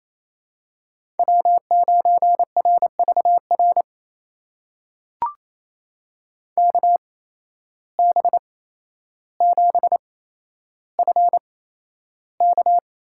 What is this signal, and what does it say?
Morse code, heard as a keyed tone, about 28 wpm, spaced as for 12 wpm, 700 Hz. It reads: W9RVR KB7FK
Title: USA Call Signs - 5 Characters - 8x Character Spacing - 28wpm